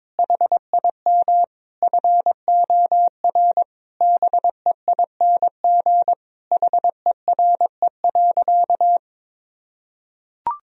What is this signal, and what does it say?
Morse code, heard as a keyed tone, about 22 wpm, 700 Hz. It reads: HIM FOR BEING HERE.
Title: Sentences from Top 1000 Words - Review Part 3 - 22wpm